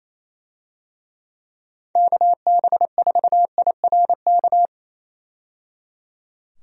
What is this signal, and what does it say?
Morse code, heard as a keyed tone, about 28 wpm, 700 Hz. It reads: KB4IRK
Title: USA Call Signs - 6 Characters - 28wpm